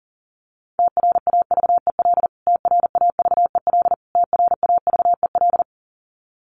Morse code, heard as a keyed tone, about 40 wpm, 700 Hz. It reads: TRAVEL TRAVEL TRAVEL
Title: Top 500 Words Repeated 40wpm